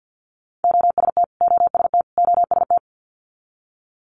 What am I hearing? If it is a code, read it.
Morse code, heard as a keyed tone, about 50 wpm, 700 Hz. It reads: OST OST OST
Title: Multi-letter Phonemes Repeated 3x 50wpm